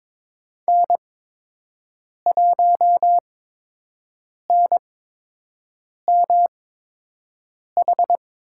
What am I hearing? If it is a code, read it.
Morse code, heard as a keyed tone, about 22 wpm, spaced as for 6 wpm, 700 Hz. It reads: N1NMH
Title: USA Call Signs - 5 Characters - 8x Character Spacing - 22wpm